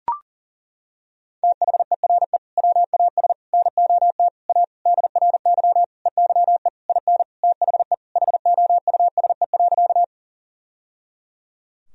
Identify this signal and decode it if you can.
Morse code, heard as a keyed tone, about 37 wpm, 700 Hz. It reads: THERE WAS NOT A DRY EYE IN THE HOUSE.
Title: Sentences from Top 1000 Words - Review Part 3 - 40wpm